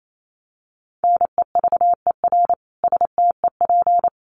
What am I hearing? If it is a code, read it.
Morse code, heard as a keyed tone, about 28 wpm, 700 Hz. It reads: NEVER STEP
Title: A Sound Of Thunder 28wpm